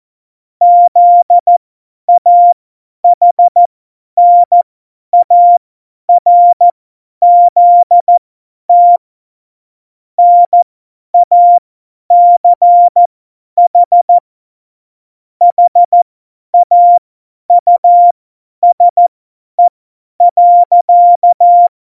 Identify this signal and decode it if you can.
Morse code, heard as a keyed tone, about 14 wpm, spaced as for 10 wpm, 700 Hz. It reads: ZAHNARZT NACH HAUSE.
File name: Morsekurs / Witze / Witz 26. Dezember 2017, 50 F70